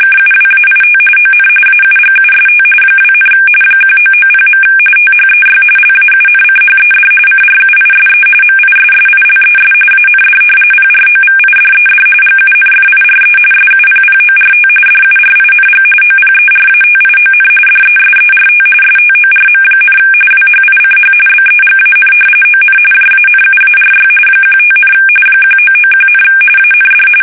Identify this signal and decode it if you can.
RTTY, about 45.45 baud. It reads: KUWAIT ON AUGUST 2, 1990. THEY LAUNCHED CRUISE MISSILES AND PROVIDED GUNFIRE SUPPORT DURING OPERATION DESERT STORM, WHICH BEGAN ON JANUARY 15, 1991, AND
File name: fsk850 / battleshipnj